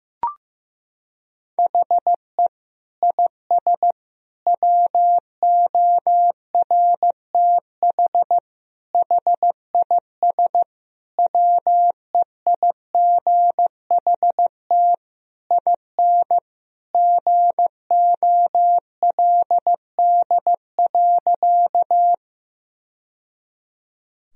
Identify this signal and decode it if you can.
Morse code, heard as a keyed tone, about 15 wpm, 700 Hz. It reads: HE IS WORTH HIS WEIGHT IN GOLD.
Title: Sentences from Top 1000 Words - Review Part 3 - 15wpm